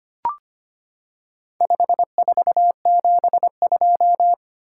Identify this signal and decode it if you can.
Morse code, heard as a keyed tone, about 25 wpm, 700 Hz. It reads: 5472
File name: practice / 25wpm / Four Letter-Number Combinations - 25wpm